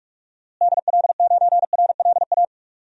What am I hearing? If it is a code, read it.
Morse code, heard as a keyed tone, about 46 wpm, 700 Hz. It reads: DL9RLA
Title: DXCC-100 Call Signs - 6 Characters - Repeated 3x - 45wpm